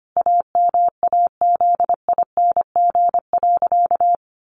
Morse code, heard as a keyed tone, about 25 wpm, 700 Hz. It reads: AMAZING.
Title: Sentences from Top 1500 Words 25wpm